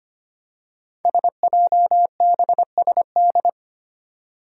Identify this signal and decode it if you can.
Morse code, heard as a keyed tone, about 25 wpm, 700 Hz. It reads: SJBSD
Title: Five Challenging Letter-Number Combinations 25wpm